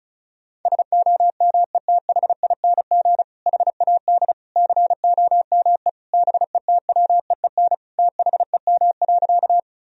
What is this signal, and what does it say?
Morse code, heard as a keyed tone, about 36 wpm, 700 Hz. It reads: SOMETHING HAD COME BETWEEN THEM.
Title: Sentences from Top 1000 Words - Review Part 1 - 35wpm